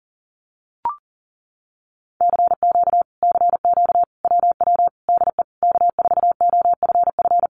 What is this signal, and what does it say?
Morse code, heard as a keyed tone, about 40 wpm, 700 Hz. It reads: CQ CQ WW DE K4OFF
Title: Common QSO Phrases Encore 40wpm